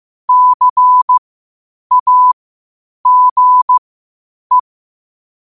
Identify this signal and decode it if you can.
Morse code, heard as a keyed tone, about 15 wpm, spaced as for 8 wpm, 1000 Hz. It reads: CAGE
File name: qotdfiles / 8wpm